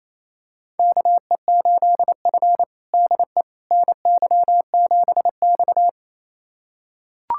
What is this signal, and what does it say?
Morse code, heard as a keyed tone, about 28 wpm, 700 Hz. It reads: KE8F DE NY7X
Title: USA Callsign DE Callsign - 4 Characters - 28wpm